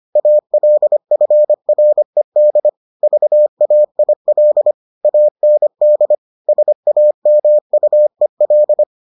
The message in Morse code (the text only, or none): ALFRED VAIL AND SAMUEL